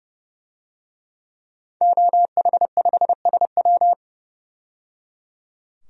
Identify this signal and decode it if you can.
Morse code, heard as a keyed tone, about 30 wpm, 700 Hz. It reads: OH5SW
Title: DXCC-100 Call Signs - 5 Characters - 30wpm